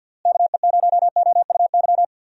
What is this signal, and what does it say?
Morse code, heard as a keyed tone, about 50 wpm, 700 Hz. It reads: KE0OUY
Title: USA Call Signs - 6 Characters - Repeated 3x - 50wpm